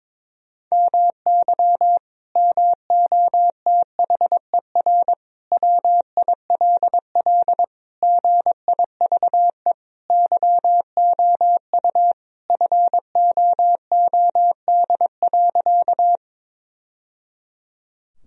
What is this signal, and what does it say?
Morse code, heard as a keyed tone, about 22 wpm, 700 Hz. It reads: MY MOTHER WILL GIVE YOU FOOD.
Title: Sentences from Top 1000 Words - Review Part 1 - 22wpm